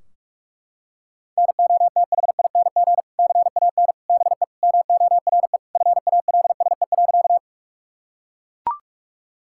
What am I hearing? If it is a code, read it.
Morse code, heard as a keyed tone, about 41 wpm, 700 Hz. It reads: NOTHING CAN BE MORE FALSE.
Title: Sentences from Top 1600 Words 45wpm